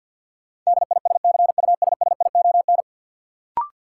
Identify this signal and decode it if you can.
Morse code, heard as a keyed tone, about 53 wpm, 700 Hz. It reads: DISCUSSION
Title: Sentences from Top 1100 Words - New Word Drill Alt - 50wpm